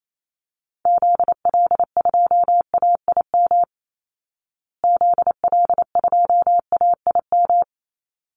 Morse code, heard as a keyed tone, about 28 wpm, 700 Hz. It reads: ZL2AIM ZL2AIM
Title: Call Signs Repeated 3x - 28wpm